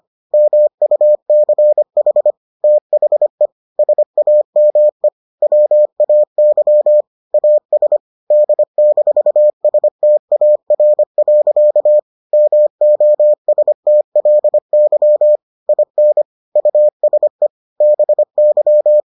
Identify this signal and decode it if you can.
Morse code, heard as a keyed tone, about 25 wpm, 600 Hz. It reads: MUCH THE SAME WAY AS D-STAR. MOSTLY IN USE BY